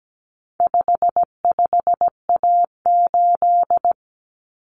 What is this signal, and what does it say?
Morse code, heard as a keyed tone, about 17 wpm, 700 Hz. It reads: 55A8
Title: Common Cut Numbers - 4-Digit 17wpm